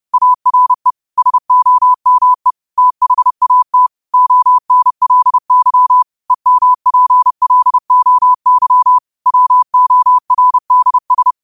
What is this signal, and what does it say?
Morse code, heard as a keyed tone, about 30 wpm, 1000 Hz. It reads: ARE SOME THAT ONLY EMPLOY WORDS